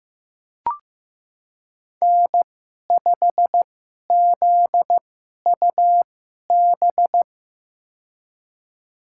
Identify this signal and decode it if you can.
Morse code, heard as a keyed tone, about 15 wpm, spaced as for 11 wpm, 700 Hz. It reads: N5ZUB
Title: USA Call Signs - 2x Character Spacing - 15wpm